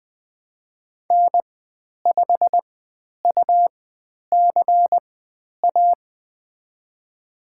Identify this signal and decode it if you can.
Morse code, heard as a keyed tone, about 20 wpm, spaced as for 10 wpm, 700 Hz. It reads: N5UCA
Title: 102 Lesson 17g - Letter u - Call Signs 20wpm